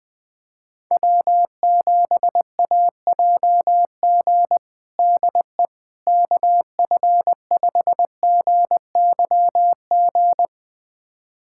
Morse code, heard as a keyed tone, about 21 wpm, 700 Hz. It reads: W7AJG DE KF5GYG
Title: USA Callsign DE Callsign 20wpm